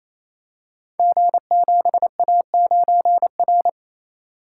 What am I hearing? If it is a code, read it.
Morse code, heard as a keyed tone, about 27 wpm, 700 Hz. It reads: G7A9R
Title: Code Groups - 5 Letter:Numbers - 28wpm